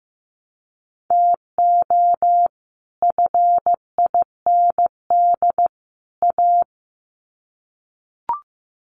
Morse code, heard as a keyed tone, about 15 wpm, 700 Hz. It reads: TO FIND A
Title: Sets of 3 Words 15wpm